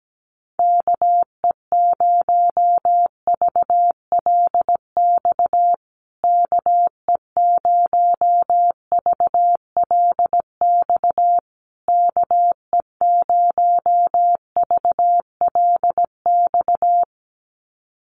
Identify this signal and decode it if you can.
Morse code, heard as a keyed tone, about 17 wpm, 700 Hz. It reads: KE0VLX KE0VLX KE0VLX
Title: USA Call Signs Repeated 3x 17wpm